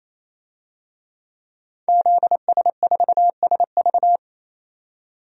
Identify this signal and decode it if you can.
Morse code, heard as a keyed tone, about 28 wpm, 700 Hz. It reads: ZS4SV
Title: DXCC-100 Call Signs - 5 Characters - 28wpm